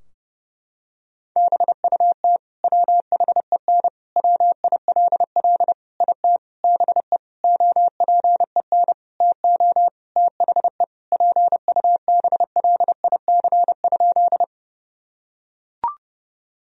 Morse code, heard as a keyed tone, about 28 wpm, 700 Hz. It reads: BUT WHEN WILL IT BE OPEN TO THE PUBLIC?
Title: Sentences from Top 1000 Words - Review Part 3 - 30wpm